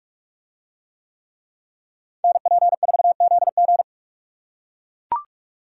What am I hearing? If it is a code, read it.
Morse code, heard as a keyed tone, about 43 wpm, 700 Hz. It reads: NP4ZG